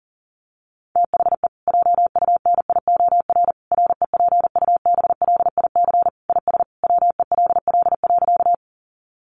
Morse code, heard as a keyed tone, about 40 wpm, 700 Hz. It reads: THE JUNIOR REPUBLIC IS WELL.
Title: Sentences from Top 1500 Words 40wpm